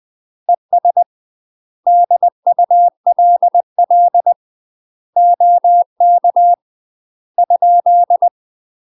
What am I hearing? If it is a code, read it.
Morse code, heard as a keyed tone, about 20 wpm, 700 Hz. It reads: ES DULL OK ?
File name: UK MASTER QSOs TRAINING / 20WPM / UK MASTER QSOs 0057